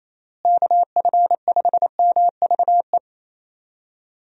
Morse code, heard as a keyed tone, about 28 wpm, 700 Hz. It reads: KF5MVE
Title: USA Call Signs - 6 Characters - Repeated 3x - 28wpm